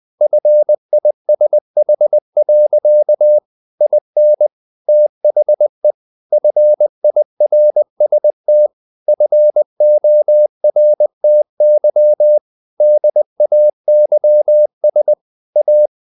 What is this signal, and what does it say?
Morse code, heard as a keyed tone, about 20 wpm, 600 Hz. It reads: FISH. IN THE FIRST FORTY DAYS A
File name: the old man and the sea / the old man and the sea part 1 20wpm